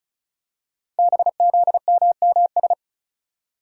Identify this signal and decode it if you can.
Morse code, heard as a keyed tone, about 36 wpm, 700 Hz. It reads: BZMMS
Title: Code Groups - 5 Letters - 35wpm